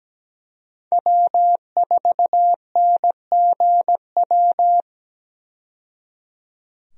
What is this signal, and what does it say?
Morse code, heard as a keyed tone, about 17 wpm, 700 Hz. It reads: W4NGW
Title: USA Call Signs - 5 Characters - 17wpm